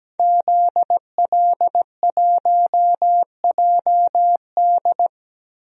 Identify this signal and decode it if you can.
Morse code, heard as a keyed tone, about 17 wpm, 700 Hz. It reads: ZL1JD
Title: DXCC-100 Call Signs - 5 Characters - Repeated 3x - 17wpm